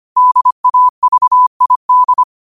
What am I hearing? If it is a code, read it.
Morse code, heard as a keyed tone, about 25 wpm, 1000 Hz. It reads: DAVID
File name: qotdfiles / 25wpm